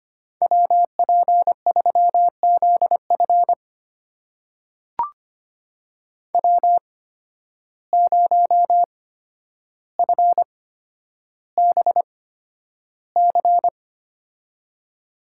Morse code, heard as a keyed tone, about 25 wpm, spaced as for 11 wpm, 700 Hz. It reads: WP3ZF W0FBC
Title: USA Call Signs - 5 Characters - 8x Character Spacing - 25wpm